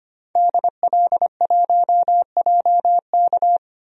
Morse code, heard as a keyed tone, about 25 wpm, 700 Hz. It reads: DL1JK
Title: DXCC-100 Call Signs - 5 Characters - 25wpm